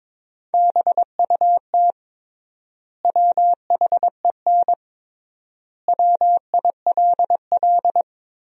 Morse code, heard as a keyed tone, about 21 wpm, 700 Hz. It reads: BUT WHEN WILL
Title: 3x Word-Spacing - Sentences from Top 100 Words - 22wpm